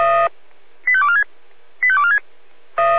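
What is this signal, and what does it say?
Five-tone selective call (ZVEI), heard as quick runs of five tones, tones of about 70 ms. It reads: 85246 85246
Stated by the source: ZVEI five-tone call received by SDR.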